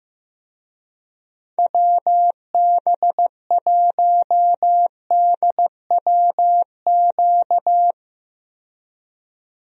Morse code, heard as a keyed tone, about 15 wpm, 700 Hz. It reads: WB1DWQ
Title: USA Call Signs - 6 Characters - 15wpm